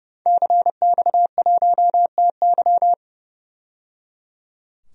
Morse code, heard as a keyed tone, about 30 wpm, 700 Hz. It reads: CX1TY
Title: DXCC-100 Call Signs - 5 Characters - 30wpm